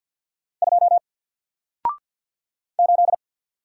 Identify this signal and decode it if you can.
Morse code, heard as a keyed tone, about 48 wpm, 700 Hz. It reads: 2 8
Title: Single Character (Letter:Number:Punctuation:Prosign) 50wpm